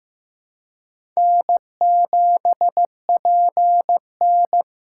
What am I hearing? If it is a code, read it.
Morse code, heard as a keyed tone, about 15 wpm, 700 Hz. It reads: N7PN
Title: USA Call Signs - 4 Characters - 2x Character Spacing - 15wpm